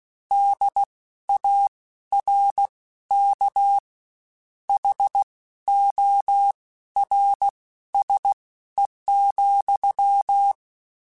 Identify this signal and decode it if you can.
Morse code, heard as a keyed tone, about 16 wpm, spaced as for 12 wpm, 800 Hz. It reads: DARK HORSE,
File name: Misc / Misc-mp3s / WordPairs sp @15